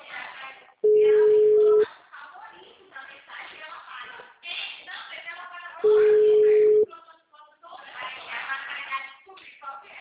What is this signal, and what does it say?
Ringback tone (Europe, 425 Hz), with speech.
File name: out- -2004- - - .14